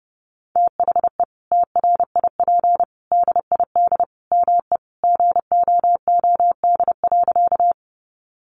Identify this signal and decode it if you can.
Morse code, heard as a keyed tone, about 30 wpm, 700 Hz. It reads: THE TRIP DID ME GOOD.
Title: Sentences from Top 1300 Words 30wpm